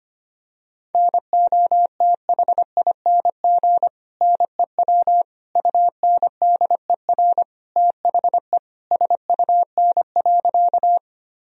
Morse code, heard as a keyed tone, about 25 wpm, 700 Hz. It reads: NOTHING NEW UNDER THE SUN.
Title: Sentences from Top 1000 Words - Review Part 1 - 25wpm